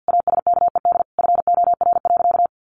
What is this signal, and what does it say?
Morse code, heard as a keyed tone, about 50 wpm, 700 Hz. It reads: ASKED FOR.
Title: A Sound Of Thunder 50wpm